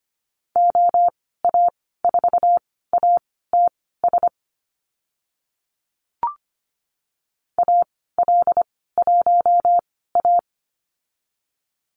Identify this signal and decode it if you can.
Morse code, heard as a keyed tone, about 25 wpm, spaced as for 16 wpm, 700 Hz. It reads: OA4ATS AL1A
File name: Lesson 15g - Number 5 - Call Signs 25wpm